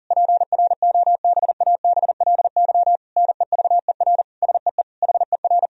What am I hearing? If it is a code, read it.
Morse code, heard as a keyed tone, about 37 wpm, 700 Hz. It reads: PROBABLY NEVER SEE HER
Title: Sentences from Top 1000 Words - Review Part 4 - 40wpm